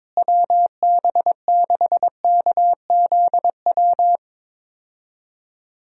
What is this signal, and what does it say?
Morse code, heard as a keyed tone, about 22 wpm, 700 Hz. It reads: WB6KZW
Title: USA Call Signs - 6 Characters - Repeated 3x - 22wpm